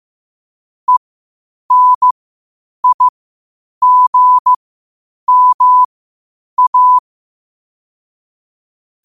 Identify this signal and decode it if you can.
Morse code, heard as a keyed tone, about 15 wpm, spaced as for 8 wpm, 1000 Hz. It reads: ENIGMA
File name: qotdfiles / 8wpm